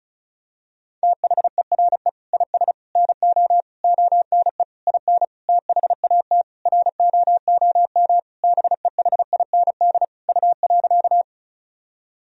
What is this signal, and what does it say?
Morse code, heard as a keyed tone, about 34 wpm, 700 Hz. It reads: THERE IS NO ONE IN THAT ROOM BEHIND U.
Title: Sentences from Top 500 Words and Abbreviations 35wpm